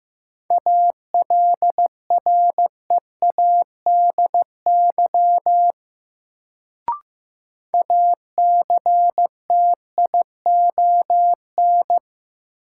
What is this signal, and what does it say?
Morse code, heard as a keyed tone, about 15 wpm, 700 Hz. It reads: ALREADY ACTION
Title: Sentences from Top 700 Words - New Word Drill - 15wpm